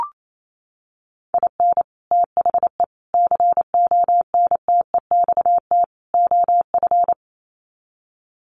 Morse code, heard as a keyed tone, about 28 wpm, 700 Hz. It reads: IN THE CONTEXT OF
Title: Sets of 4 Words 28wpm